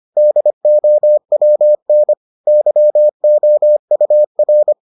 Morse code, heard as a keyed tone, about 25 wpm, 600 Hz. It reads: DOWN YOUR